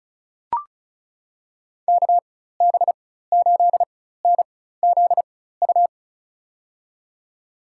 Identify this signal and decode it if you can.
Morse code, heard as a keyed tone, about 34 wpm, spaced as for 16 wpm, 700 Hz. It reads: KB8NZU
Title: USA Call Signs - 6 Characters - 4x Character Spacing - 35wpm